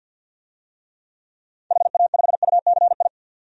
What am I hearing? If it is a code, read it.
Morse code, heard as a keyed tone, about 51 wpm, 700 Hz. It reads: HA5UCI
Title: DXCC-100 Call Signs - 6 Characters - 50wpm